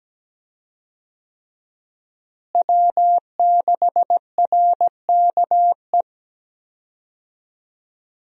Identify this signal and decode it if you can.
Morse code, heard as a keyed tone, about 17 wpm, 700 Hz. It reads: W6RKE
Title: USA Call Signs - 5 Characters - Repeated 3x - 17wpm